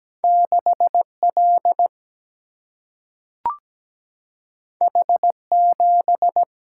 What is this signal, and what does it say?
Morse code, heard as a keyed tone, about 17 wpm, 700 Hz. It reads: 6 L H 7
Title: Single-Letter Single-Number 17wpm